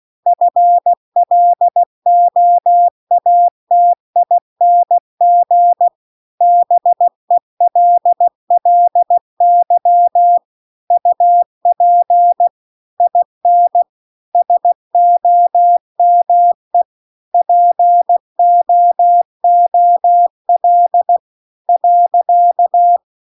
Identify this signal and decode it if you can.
Morse code, heard as a keyed tone, about 16 wpm, 700 Hz. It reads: FLOATING BELLY UP IN SOME POOL .
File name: CW-The-Big-Twoheaded-River-2-16WPM